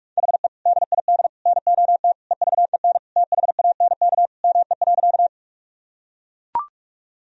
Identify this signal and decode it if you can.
Morse code, heard as a keyed tone, about 41 wpm, 700 Hz. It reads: HE DID NOT EVEN THANK ME.
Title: Sentences from Top 1000 Words - Review Part 1 - 45wpm